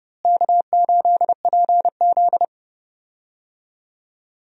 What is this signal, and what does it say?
Morse code, heard as a keyed tone, about 32 wpm, 700 Hz. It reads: K8PZ
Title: USA Call Signs - 4 Characters - Repeated 3x - 30wpm